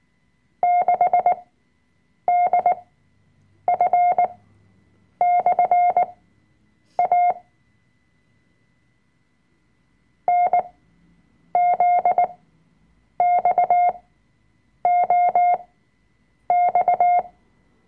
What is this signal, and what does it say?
Morse code, heard as a keyed tone, about 19 wpm, spaced as for 7 wpm, 710 Hz. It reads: 6DF/A NZXOX
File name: Workshop Dateien / CW-Online-Kurs / Lektion 005